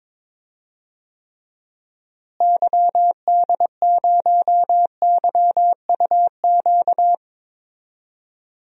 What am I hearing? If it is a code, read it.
Morse code, heard as a keyed tone, about 22 wpm, 700 Hz. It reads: YD0YUQ
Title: DXCC-100 Call Signs - 6 Characters - Repeated 2x - 22wpm